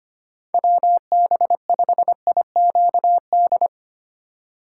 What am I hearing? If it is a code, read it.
Morse code, heard as a keyed tone, about 25 wpm, 700 Hz. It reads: WB5IQD
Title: USA Call Signs - 8x Character Spacing - 25wpm